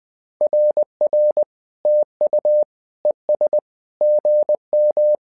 Morse code, heard as a keyed tone, about 20 wpm, 600 Hz. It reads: RR TU ES GM
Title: SOTA QSO Practice 20wpm